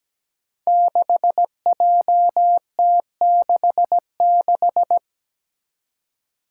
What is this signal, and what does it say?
Morse code, heard as a keyed tone, about 17 wpm, 700 Hz. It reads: 6JT66
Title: Five Challenging Letter-Number Combinations 17wpm